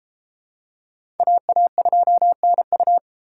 Morse code, heard as a keyed tone, about 32 wpm, 700 Hz. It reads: AA2NU